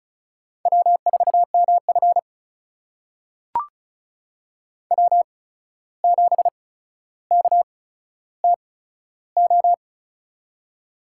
Morse code, heard as a keyed tone, about 36 wpm, spaced as for 14 wpm, 700 Hz. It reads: W4MF W7KTO
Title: USA Call Signs - 8x Character Spacing - 35wpm